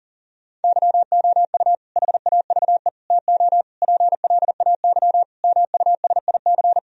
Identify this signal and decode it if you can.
Morse code, heard as a keyed tone, about 40 wpm, 700 Hz. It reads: YOU HAVE TO PLAY MUSIC
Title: Sentences from Top 1000 Words - Review Part 1 - 40wpm